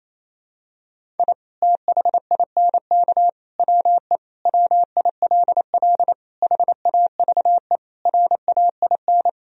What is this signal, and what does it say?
Morse code, heard as a keyed tone, about 28 wpm, 700 Hz. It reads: I THINK WE WILL HAVE RAIN
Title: Sentences from Top 1000 Words - Review Part 1 - 28wpm